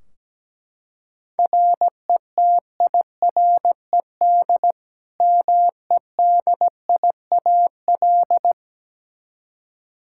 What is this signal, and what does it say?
Morse code, heard as a keyed tone, about 17 wpm, 700 Hz. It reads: RETIRED MEDIAL
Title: Practice QSO - Exchange 3 - More Info - 17wpm